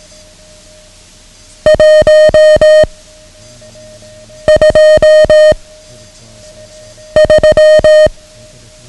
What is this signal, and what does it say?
Morse code, heard as a keyed tone, about 18 wpm, 620 Hz. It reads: 1 2 3
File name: ham radio / CodeQuick / Code Quick 09